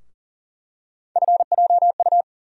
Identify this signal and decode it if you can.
Morse code, heard as a keyed tone, about 40 wpm, 700 Hz. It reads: FJU
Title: USA Call Sign Postfixes - 3 Letters 40wpm